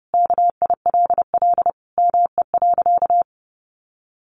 Morse code, heard as a keyed tone, about 28 wpm, 700 Hz. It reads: KILL ME.
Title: Sentences from Top 1400 Words 30wpm